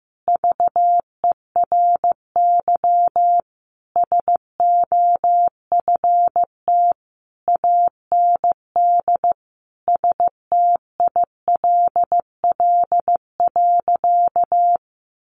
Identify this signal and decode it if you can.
Morse code, heard as a keyed tone, about 15 wpm, 700 Hz. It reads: VERY SOFT AND STILL.
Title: Sentences from Top 1500 Words 15wpm